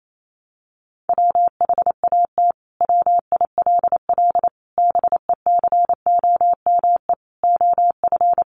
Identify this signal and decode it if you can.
Morse code, heard as a keyed tone, about 28 wpm, 700 Hz. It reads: WHAT WILL BECOME OF
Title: Sentences from Top 600 Words 28wpm